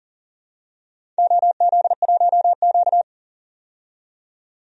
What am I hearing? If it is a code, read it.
Morse code, heard as a keyed tone, about 40 wpm, 700 Hz. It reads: OZ1Q
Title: DXCC-100 Call Signs - 4 Characters - 40wpm